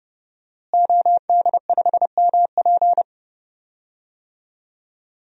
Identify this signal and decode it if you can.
Morse code, heard as a keyed tone, about 30 wpm, 700 Hz. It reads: OD5MP
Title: DXCC-100 Call Signs - 5 Characters - Repeated 2x - 30wpm